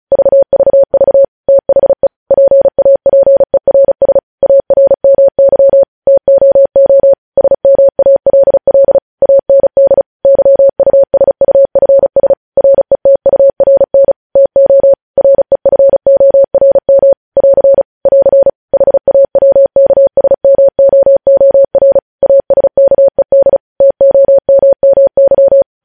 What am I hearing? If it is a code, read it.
Morse code, heard as a keyed tone, about 36 wpm, 550 Hz. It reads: VVV THE PAPERS ARMY TOO SMALL AND YUSUFS RETURN TO REFORM <AR> <AR> HAWKSMOOR ASKED TOMMY
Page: news headlines in morse code at 35 wpm